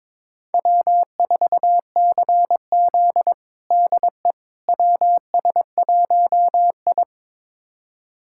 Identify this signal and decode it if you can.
Morse code, heard as a keyed tone, about 22 wpm, 700 Hz. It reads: W4CZ DE WS1I
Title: USA Callsign DE Callsign - 4 Characters - 22wpm